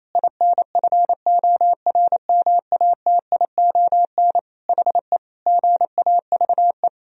Morse code, heard as a keyed tone, about 28 wpm, 700 Hz. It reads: INFORMATION HE GAVE
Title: Sentences from Top 1000 Words - Review Part 3 - 28wpm